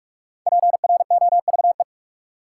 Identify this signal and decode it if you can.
Morse code, heard as a keyed tone, about 44 wpm, 700 Hz. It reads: PROVE
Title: Sentences from Top 1600 Words - New Word Drill - 45wpm